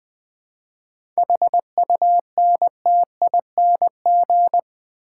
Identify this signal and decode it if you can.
Morse code, heard as a keyed tone, about 20 wpm, 700 Hz. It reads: HUNTING
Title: 7 Letter Words - First 3 Letters Given - 20wpm